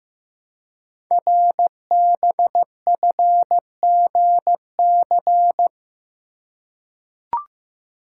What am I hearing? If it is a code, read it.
Morse code, heard as a keyed tone, about 15 wpm, 700 Hz. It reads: RBFGC
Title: Code Groups - 5 Letters - 15wpm